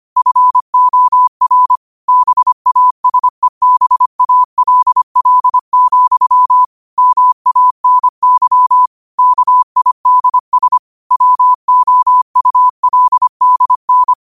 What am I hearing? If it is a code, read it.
Morse code, heard as a keyed tone, about 25 wpm, 1000 Hz. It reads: FOR BASEBALL, MANY KIDS WOULDN
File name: qotdfiles / 25wpm